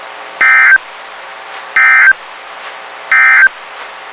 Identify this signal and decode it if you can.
Emergency Alert System (SAME) tones, heard as harsh buzzing bursts. It is heard from 0.3 to 3.6 s.
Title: WXK91 NOAA Weather Radio